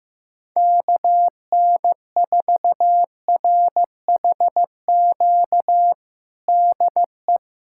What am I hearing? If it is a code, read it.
Morse code, heard as a keyed tone, about 15 wpm, 700 Hz. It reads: KN4RHQ DE
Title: USA Callsign DE Callsign - 6 Characters - 15wpm